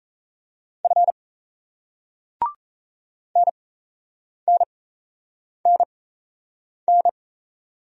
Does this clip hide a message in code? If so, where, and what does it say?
Yes: Morse code from 0.8 to 7.1 s, heard as a keyed tone, about 38 wpm, spaced as for 8 wpm, 700 Hz. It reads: F NNNN